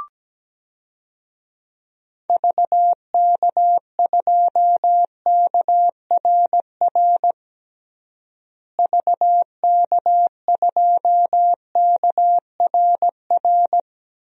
Morse code, heard as a keyed tone, about 17 wpm, 700 Hz. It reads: VK2KRR VK2KRR
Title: DXCC-100 Call Signs - 6 Characters - Repeated 2x - 17wpm